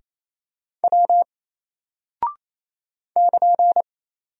Morse code, heard as a keyed tone, about 27 wpm, 700 Hz. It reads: W <KN>
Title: Single Character (Letter:Number:Punctuation:Prosign) 28wpm